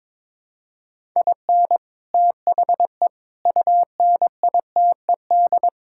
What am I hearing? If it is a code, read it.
Morse code, heard as a keyed tone, about 23 wpm, 700 Hz. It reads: IN THE UNITED
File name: practice / 22wpm / Build A Phrase - 5-Word Phrases - 22wpm